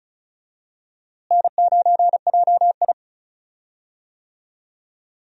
Morse code, heard as a keyed tone, about 36 wpm, 700 Hz. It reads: N9JI
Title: USA Call Signs - 4 Characters - Repeated 3x - 35wpm